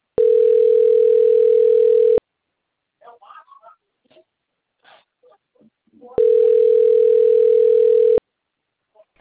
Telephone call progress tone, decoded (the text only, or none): ringback tone